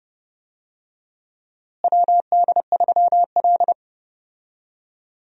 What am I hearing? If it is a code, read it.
Morse code, heard as a keyed tone, about 30 wpm, 700 Hz. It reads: WD3L
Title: USA Call Signs - 4 Characters - Repeated 2x - 30wpm